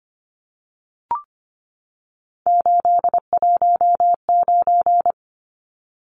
Morse code, heard as a keyed tone, about 25 wpm, 690 Hz. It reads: 819
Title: Numbers - 3-Digit 25wpm